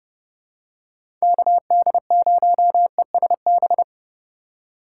Morse code, heard as a keyed tone, about 30 wpm, 700 Hz. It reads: KD0ESB
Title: USA Call Signs - 6 Characters - 2x Character Spacing - 30wpm